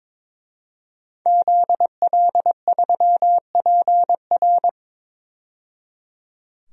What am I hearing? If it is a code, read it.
Morse code, heard as a keyed tone, about 21 wpm, 700 Hz. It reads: ZL3PR
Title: DXCC-100 Call Signs - 5 Characters - 22wpm